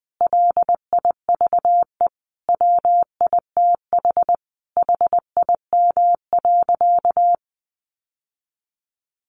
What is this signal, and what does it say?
Morse code, heard as a keyed tone, about 20 wpm, 700 Hz. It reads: LIVE WITH HIM.
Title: Sentences from Top 1100 Words 20wpm